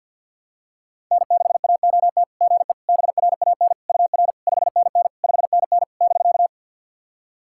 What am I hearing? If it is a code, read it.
Morse code, heard as a keyed tone, about 55 wpm, 700 Hz. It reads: N6AOT GE BRAN UR 5NN 5NN <BK>
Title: SOTA QSO Practice 50wpm